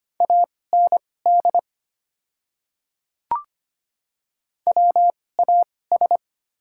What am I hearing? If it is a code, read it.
Morse code, heard as a keyed tone, about 25 wpm, spaced as for 18 wpm, 690 Hz. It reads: AND WAS
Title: 2x Character Spacing - Top 100 Words - 25wpm